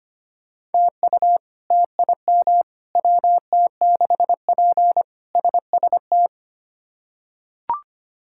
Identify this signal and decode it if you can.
Morse code, heard as a keyed tone, about 25 wpm, 700 Hz. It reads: TU TIM WT6P SST
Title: K1USN SST QSO Practice - 25wpm